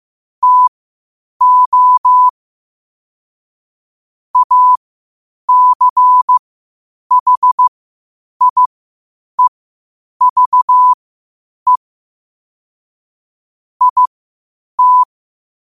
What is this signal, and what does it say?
Morse code, heard as a keyed tone, about 15 wpm, spaced as for 8 wpm, 1000 Hz. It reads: TO ACHIEVE IT
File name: qotdfiles / 8wpm